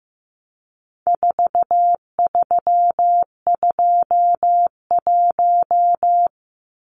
Morse code, heard as a keyed tone, about 15 wpm, 700 Hz. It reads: 4321
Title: Numbers - 4-Digit 15wpm